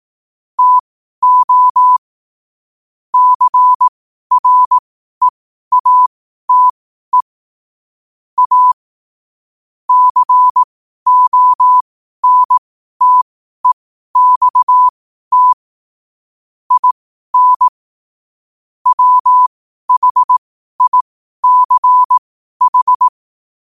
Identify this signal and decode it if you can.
Morse code, heard as a keyed tone, about 18 wpm, spaced as for 12 wpm, 1000 Hz. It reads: TO CREATE A CONTEXT IN WHICH
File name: qotdfiles / 12wpm